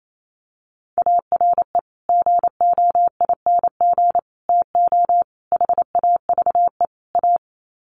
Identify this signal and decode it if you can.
Morse code, heard as a keyed tone, about 28 wpm, 700 Hz. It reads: ARE GOING TO HAVE A
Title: Sets of 5 Words Encore2 28wpm